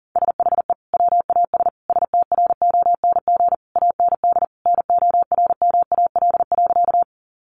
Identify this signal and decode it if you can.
Morse code, heard as a keyed tone, about 40 wpm, 700 Hz. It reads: SHE WAS STRONG AND NORMAL.
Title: Sentences from Top 1200 Words 40wpm